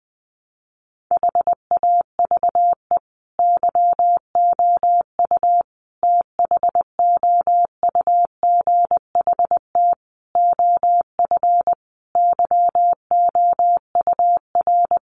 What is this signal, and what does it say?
Morse code, heard as a keyed tone, about 20 wpm, 700 Hz. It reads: HAVE YOU THOUGHT OF YOUR
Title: Sentences from Top 700 Words 20wpm